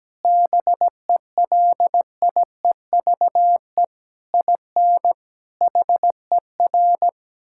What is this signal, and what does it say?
Morse code, heard as a keyed tone, about 17 wpm, 700 Hz. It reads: BELIEVE IN HER
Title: Sentences from Top 1600 Words 17wpm